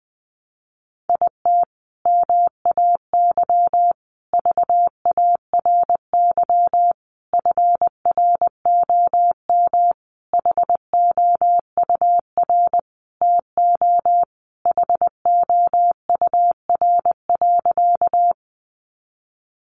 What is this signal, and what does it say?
Morse code, heard as a keyed tone, about 19 wpm, 700 Hz. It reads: IT MAY VARY FROM HOUR TO HOUR.
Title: Sentences from Top 1200 Words 20wpm